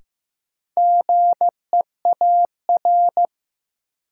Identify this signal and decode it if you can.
Morse code, heard as a keyed tone, about 15 wpm, 700 Hz. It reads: GEAR
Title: Sentences from Top 1400 Words - New Word Drill - 15wpm